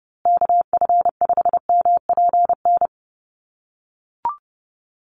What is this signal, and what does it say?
Morse code, heard as a keyed tone, about 30 wpm, 700 Hz. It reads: KF5MPN
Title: USA Call Signs Repeated 2x 30wpm